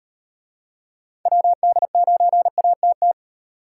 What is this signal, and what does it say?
Morse code, heard as a keyed tone, about 37 wpm, 700 Hz. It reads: WD9ATT